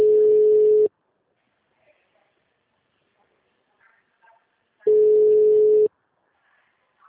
Ringback tone (Europe, 425 Hz).